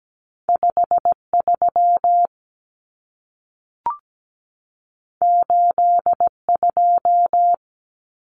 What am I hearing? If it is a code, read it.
Morse code, heard as a keyed tone, about 17 wpm, 700 Hz. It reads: 53 82
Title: Common Cut Numbers - 2-Digit 17wpm